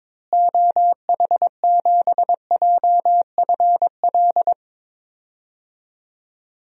Morse code, heard as a keyed tone, about 22 wpm, 700 Hz. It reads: OH7JFL
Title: DXCC-100 Call Signs - 6 Characters - Repeated 3x - 22wpm